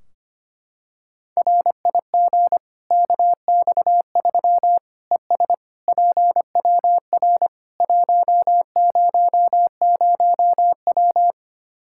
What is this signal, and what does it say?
Morse code, heard as a keyed tone, about 25 wpm, 700 Hz. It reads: RIG KX3 ES PWR 100W
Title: K1USN SST QSO Practice 25wpm